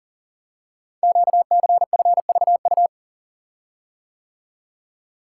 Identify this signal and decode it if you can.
Morse code, heard as a keyed tone, about 40 wpm, 700 Hz. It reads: QCFVU
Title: Code Groups - 5 Letters - 40wpm